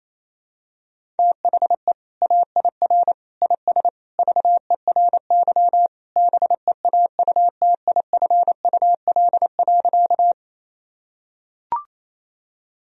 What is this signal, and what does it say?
Morse code, heard as a keyed tone, about 28 wpm, 700 Hz. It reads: THE AIR IS VERY BEAUTIFUL.
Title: Sentences from Top 1000 Words - Review Part 1 - 28wpm